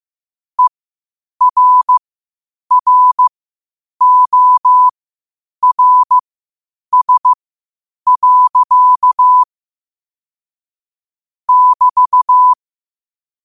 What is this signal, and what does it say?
Morse code, heard as a keyed tone, about 15 wpm, spaced as for 8 wpm, 1000 Hz. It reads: ERRORS. <BT>
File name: qotdfiles / 8wpm